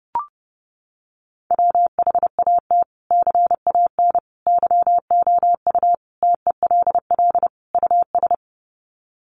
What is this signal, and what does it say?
Morse code, heard as a keyed tone, about 30 wpm, 700 Hz. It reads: WHAT CAN YOU TELL US
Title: Sets of 5 Words Encore 30wpm